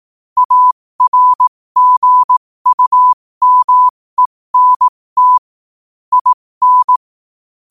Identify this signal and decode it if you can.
Morse code, heard as a keyed tone, about 18 wpm, spaced as for 15 wpm, 1000 Hz. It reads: ARGUMENT IN
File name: qotdfiles / 15wpm